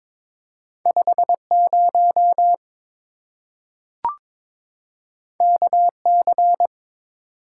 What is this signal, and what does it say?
Morse code, heard as a keyed tone, about 22 wpm, 700 Hz. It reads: 50 KC
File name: practice / 22wpm / Two Letter-Number Combinations - 22wpm